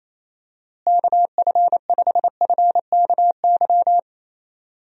Morse code, heard as a keyed tone, about 28 wpm, 700 Hz. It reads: KF5FKY
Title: USA Call Signs - 6 Characters - Repeated 2x - 28wpm